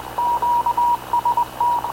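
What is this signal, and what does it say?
Morse code, heard as a keyed tone, about 20 wpm, 960 Hz. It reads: QST
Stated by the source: Havana Radio / CLA sending Spanish press (PX) in Morse, recorded by Radio Officer Paul Dumesnil onboard CNAV QUEST/CZDO.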